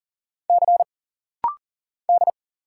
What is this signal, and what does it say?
Morse code, heard as a keyed tone, about 40 wpm, 700 Hz. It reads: C D
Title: Single Letters - Rapid-Fire 40wpm